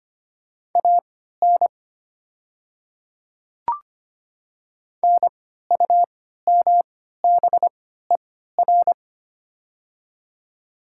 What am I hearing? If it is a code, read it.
Morse code, heard as a keyed tone, about 25 wpm, spaced as for 14 wpm, 700 Hz. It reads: AN NUMBER
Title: 3x Character Spacing - Top 100 Words - 25wpm